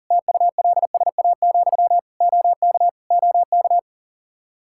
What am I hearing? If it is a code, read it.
Morse code, heard as a keyed tone, about 40 wpm, 700 Hz. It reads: TULSA, OK OK
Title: Practice QSO - Exchange 1 - Beginning - 40wpm